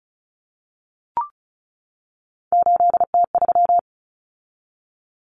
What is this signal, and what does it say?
Morse code, heard as a keyed tone, about 34 wpm, 700 Hz. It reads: 8T3
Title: Common Cut Numbers - 3-Digit 35wpm